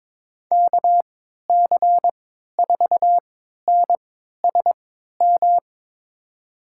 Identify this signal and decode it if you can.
Morse code, heard as a keyed tone, about 21 wpm, spaced as for 12 wpm, 700 Hz. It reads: KC4NSM
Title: USA Call Signs - 6 Characters - 3x Character Spacing - 22wpm